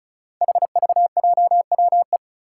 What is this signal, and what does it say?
Morse code, heard as a keyed tone, about 36 wpm, 700 Hz. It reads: HVJWE
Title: Five Challenging Letter-Number Combinations 35wpm